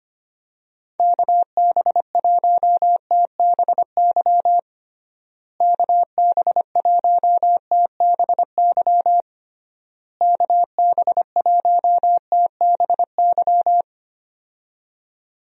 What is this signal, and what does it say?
Morse code, heard as a keyed tone, about 25 wpm, 700 Hz. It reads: KB1TBY KB1TBY KB1TBY
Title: USA Call Signs - 6 Characters - Repeated 3x - 25wpm